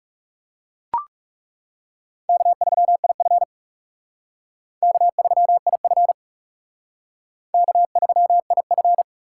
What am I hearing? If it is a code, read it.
Morse code, heard as a keyed tone, about 36 wpm, 700 Hz. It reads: K3IF K3IF K3IF
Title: Speed Racing - USA 1x2 Call Signs 30wpm